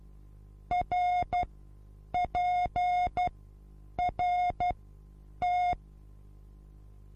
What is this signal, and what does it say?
Morse code, heard as a keyed tone, about 12 wpm, spaced as for 8 wpm, 720 Hz. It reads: RPRT
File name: Onlinelehrgang / morsen / DJ4UF Morsen23